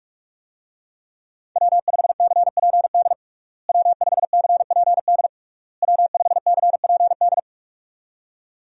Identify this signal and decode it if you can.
Morse code, heard as a keyed tone, about 46 wpm, 700 Hz. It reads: W5CPD W5CPD W5CPD
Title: USA Call Signs - 5 Characters - Repeated 3x - 45wpm